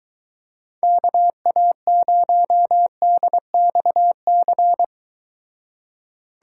Morse code, heard as a keyed tone, about 23 wpm, 700 Hz. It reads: KA0DXC